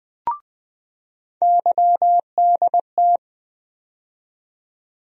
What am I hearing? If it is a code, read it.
Morse code, heard as a keyed tone, about 21 wpm, 700 Hz. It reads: YDT
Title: USA Call Sign Postfixes - 3 Letters 20wpm